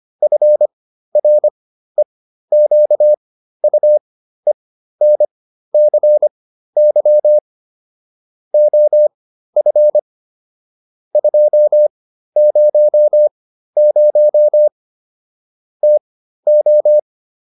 Morse code, heard as a keyed tone, about 25 wpm, spaced as for 13 wpm, 600 Hz. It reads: FREQUENCY OF 200 TO